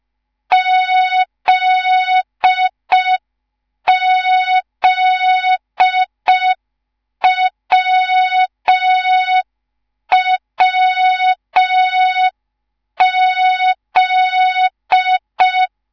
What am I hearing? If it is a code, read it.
Morse code, heard as a keyed tone, about 5 wpm, 3700 Hz. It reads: ZZWWZ
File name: Morze-HA2MN / hang / 10GWZ HA2MN